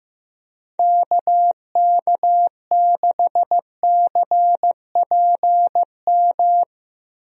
Morse code, heard as a keyed tone, about 15 wpm, 700 Hz. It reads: KK6CPM
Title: USA Call Signs - 6 Characters - 15wpm